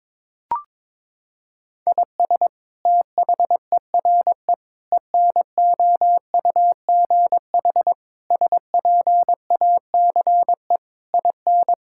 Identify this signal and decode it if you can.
Morse code, heard as a keyed tone, about 22 wpm, 700 Hz. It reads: IS THERE ENOUGH SPACE IN
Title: Sentences from Top 1000 Words - Review Part 5 - 22wpm